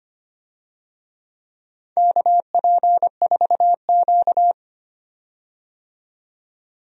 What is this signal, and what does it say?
Morse code, heard as a keyed tone, about 25 wpm, 700 Hz. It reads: KP4Q
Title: DXCC-100 Call Signs - 4 Characters - 25wpm